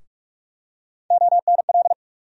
Morse code, heard as a keyed tone, about 46 wpm, 700 Hz. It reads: ONL
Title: USA Call Sign Postfixes - 3 Letters 45wpm